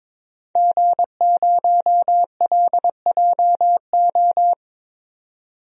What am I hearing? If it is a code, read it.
Morse code, heard as a keyed tone, about 22 wpm, 700 Hz. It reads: G0LJO
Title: DXCC-100 Call Signs - 5 Characters - 22wpm